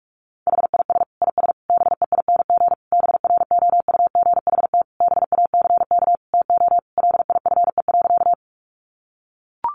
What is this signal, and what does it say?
Morse code, heard as a keyed tone, about 44 wpm, 700 Hz. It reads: HIS IS BEING BROUGHT BACK TO LIFE.
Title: Sentences from Top 600 Words 45wpm